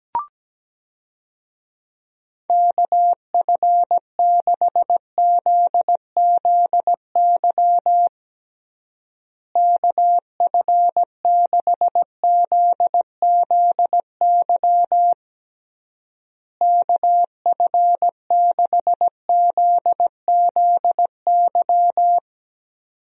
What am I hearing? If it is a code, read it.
Morse code, heard as a keyed tone, about 17 wpm, 700 Hz. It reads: KF6ZZY KF6ZZY KF6ZZY
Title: USA Call Signs - 6 Characters - Repeated 3x - 17wpm